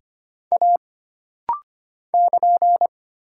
Morse code, heard as a keyed tone, about 25 wpm, 700 Hz. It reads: A <KN>
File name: Single Character (Letter:Number:Punctuation:Prosign) - Rapid-Fire - 25wpm